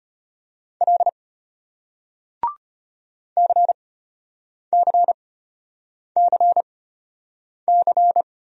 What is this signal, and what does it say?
Morse code, heard as a keyed tone, about 34 wpm, spaced as for 8 wpm, 700 Hz. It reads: L CCCC